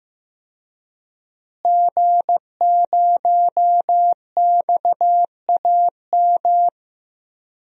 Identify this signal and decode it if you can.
Morse code, heard as a keyed tone, about 15 wpm, 700 Hz. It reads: G0XAM
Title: DXCC-100 Call Signs - 5 Characters - Repeated 3x - 15wpm